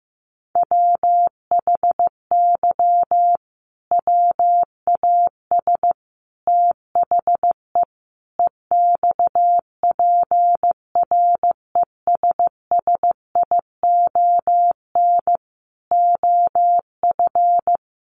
Morse code, heard as a keyed tone, about 15 wpm, 700 Hz. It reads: WHY WAS THE EXPRESSION OF
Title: Sentences from Top 1400 Words 15wpm